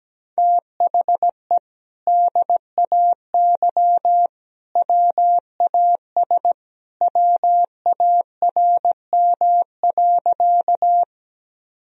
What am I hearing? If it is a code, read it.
Morse code, heard as a keyed tone, about 17 wpm, 700 Hz. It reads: THE DAY WAS WARM.
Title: Sentences from Top 1000 Words - Review Part 2 - 17wpm